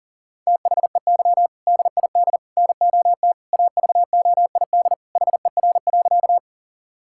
Morse code, heard as a keyed tone, about 40 wpm, 700 Hz. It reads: THEY DID NOT AVOID HER.
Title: Sentences from Top 1000 Words - Review Part 6 - 40wpm